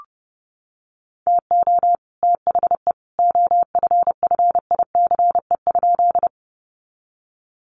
Morse code, heard as a keyed tone, about 30 wpm, 700 Hz. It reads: TO THE OFFICE?
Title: Sentences from Top 1000 Words 30wpm